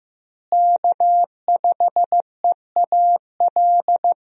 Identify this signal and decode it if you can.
Morse code, heard as a keyed tone, about 15 wpm, 700 Hz. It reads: K5EAL
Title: USA Call Signs - 5 Characters - Repeated 3x - 15wpm